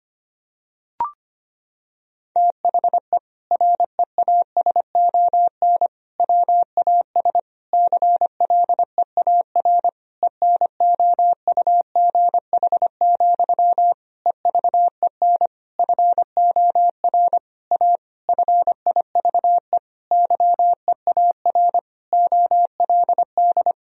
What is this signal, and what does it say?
Morse code, heard as a keyed tone, about 25 wpm, 700 Hz. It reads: THE REASON WAS CLEAR ENOUGH, EVEN FOR A FIVE YEAR OLD
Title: Sentences from Top 1000 Words - Review Part 2 - 25wpm